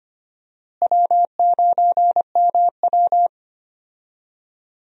Morse code, heard as a keyed tone, about 25 wpm, 700 Hz. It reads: W9MW
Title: USA Call Signs - 4 Characters - Repeated 2x - 25wpm